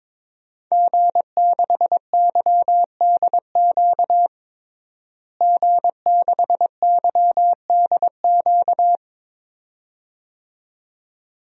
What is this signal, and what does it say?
Morse code, heard as a keyed tone, about 22 wpm, 700 Hz. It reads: G6YDQ G6YDQ
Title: DXCC-100 Call Signs - 5 Characters - Repeated 2x - 22wpm